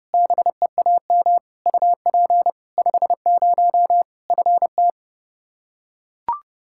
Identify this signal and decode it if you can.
Morse code, heard as a keyed tone, about 30 wpm, 700 Hz. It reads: BEAM UP 50 FT
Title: K1USN SST QSO Practice 30wpm